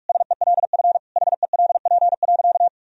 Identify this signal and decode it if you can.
Morse code, heard as a keyed tone, about 46 wpm, 700 Hz. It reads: SELF HELP.
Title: Sentences from Top 1000 Words - Review Part 1 - 45wpm